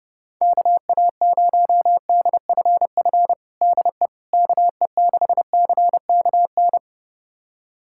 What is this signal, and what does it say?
Morse code, heard as a keyed tone, about 30 wpm, 700 Hz. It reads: KA0DFF DE KE6CKN
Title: USA Callsign DE Callsign 30wpm